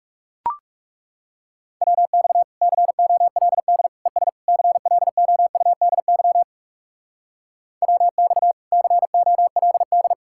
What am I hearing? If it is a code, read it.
Morse code, heard as a keyed tone, about 37 wpm, 700 Hz. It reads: WX COLD ES CLOUDY WX COLD
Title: Speed-Racing Common QSO Phrases - 30wpm